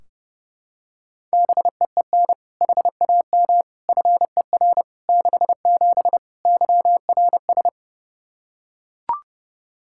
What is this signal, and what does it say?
Morse code, heard as a keyed tone, about 32 wpm, 700 Hz. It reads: BEEN HAM FER 67 YRS
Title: Practice QSO - Exchange 3 - More Info - 30wpm